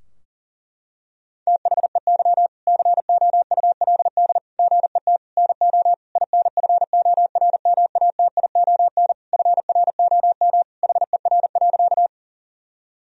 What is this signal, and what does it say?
Morse code, heard as a keyed tone, about 40 wpm, 700 Hz. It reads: THEY COULD GET NO INFORMATION FROM HER.
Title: Sentences from Top 1000 Words - Review Part 3 - 40wpm